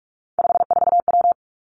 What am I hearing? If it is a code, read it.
Morse code, heard as a keyed tone, about 46 wpm, 700 Hz. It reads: 54W
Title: Three Challenging Letter-Number Combinations 45wpm